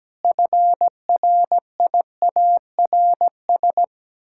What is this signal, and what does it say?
Morse code, heard as a keyed tone, about 17 wpm, 700 Hz. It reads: FRIARS
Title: Pangram Challenge v2 - 17wpm